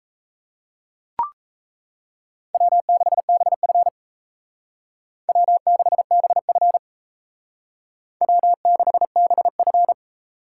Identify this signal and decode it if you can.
Morse code, heard as a keyed tone, about 34 wpm, 700 Hz. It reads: W6BF W6BF W6BF
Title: Speed Racing - USA 1x2 Call Signs 28wpm